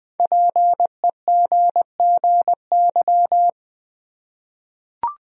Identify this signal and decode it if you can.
Morse code, heard as a keyed tone, about 20 wpm, 700 Hz. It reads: PEGGY
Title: USA Common Names - Max 5 Letters - 20wpm